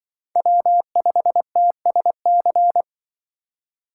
Morse code, heard as a keyed tone, about 24 wpm, 700 Hz. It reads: W5TSC
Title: Speed Racing - USA 1x3 Call Signs 20wpm